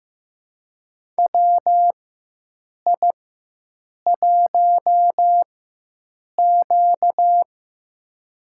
Morse code, heard as a keyed tone, about 15 wpm, spaced as for 7 wpm, 700 Hz. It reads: WI1Q
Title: USA Call Signs - 4 Characters - 4x Character Spacing - 15wpm